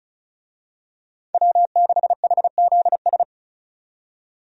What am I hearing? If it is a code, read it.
Morse code, heard as a keyed tone, about 36 wpm, 700 Hz. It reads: W6HZS
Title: USA Call Signs - 5 Characters - 35wpm